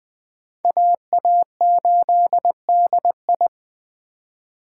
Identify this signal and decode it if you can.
Morse code, heard as a keyed tone, about 24 wpm, 700 Hz. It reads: AA8DI
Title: Speed Racing - USA 2x2 Call Signs 20wpm